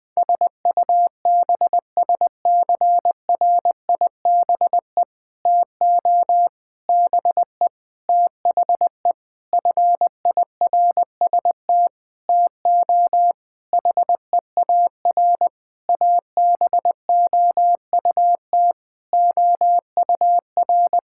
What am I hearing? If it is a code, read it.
Morse code, heard as a keyed tone, about 20 wpm, 700 Hz. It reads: SUBSCRIBE TO BE THE FIRST TO HEAR ABOUT OUR
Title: Sentences with 6-Word Phrases - 20wpm